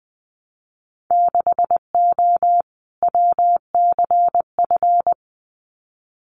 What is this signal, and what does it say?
Morse code, heard as a keyed tone, about 20 wpm, 700 Hz. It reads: 6O WCF
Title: Winter Field Day Exchanges Encore 20wpm